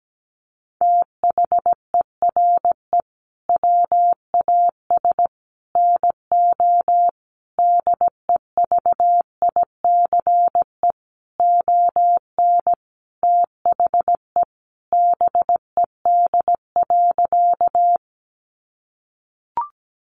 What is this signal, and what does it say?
Morse code, heard as a keyed tone, about 17 wpm, 700 Hz. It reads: THERE WAS NO DEVICE ON THE BED.
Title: Sentences from Top 1500 Words 17wpm